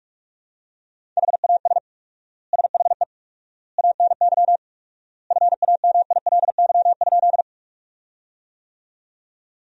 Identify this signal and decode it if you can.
Morse code, heard as a keyed tone, about 46 wpm, 700 Hz. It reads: HAS SHE ANY FAMILY?
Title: 4x Word-Spacing - Sentences from Top 400 Words - 45wpm